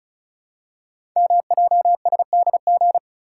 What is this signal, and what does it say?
Morse code, heard as a keyed tone, about 36 wpm, 700 Hz. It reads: MJSDG
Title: Code Groups - 5 Letters - 35wpm